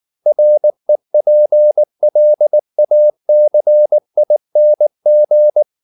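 Morse code, heard as a keyed tone, about 19 wpm, 600 Hz. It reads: REPLACING
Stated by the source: Morse code at 19 wpm from CW med Gnister, reading Ken Follett's novel Never in English.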